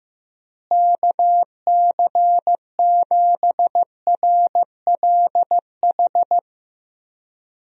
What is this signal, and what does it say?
Morse code, heard as a keyed tone, about 15 wpm, 700 Hz. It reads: KC7RLH
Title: USA Call Signs - 6 Characters - Repeated 3x - 15wpm